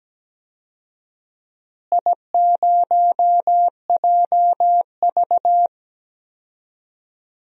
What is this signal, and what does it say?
Morse code, heard as a keyed tone, about 17 wpm, 700 Hz. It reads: I0JV
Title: DXCC-100 Call Signs - 4 Characters - Repeated 2x - 17wpm